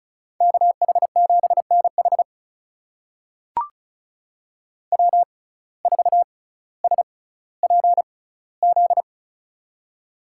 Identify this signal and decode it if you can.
Morse code, heard as a keyed tone, about 36 wpm, spaced as for 18 wpm, 700 Hz. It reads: KH7NH W4SPZ
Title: USA Call Signs - 6x Character Spacing - 35wpm